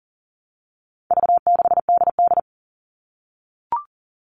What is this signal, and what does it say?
Morse code, heard as a keyed tone, about 40 wpm, 700 Hz. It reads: V6DD
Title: All Cut Numbers - 4-Digit 40wpm